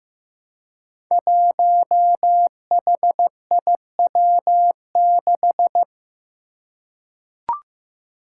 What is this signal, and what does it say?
Morse code, heard as a keyed tone, about 15 wpm, 700 Hz. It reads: 1HIW6
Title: Five Challenging Letter-Number Combinations 15wpm